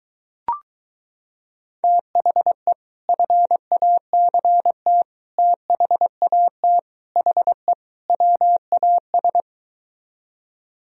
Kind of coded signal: Morse code

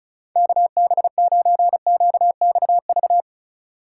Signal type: Morse code

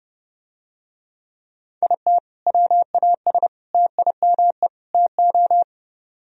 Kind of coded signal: Morse code

